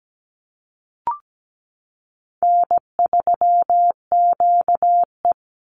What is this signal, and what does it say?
Morse code, heard as a keyed tone, about 17 wpm, 700 Hz. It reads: N3QE
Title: Easy Call Signs Repeated 2x - 17wpm